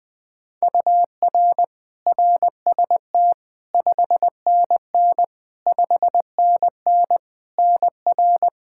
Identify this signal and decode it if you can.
Morse code, heard as a keyed tone, about 20 wpm, 700 Hz. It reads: UR RST 5NN 5NN NR
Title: Practice QSO - Exchange 1 - Beginning - 20wpm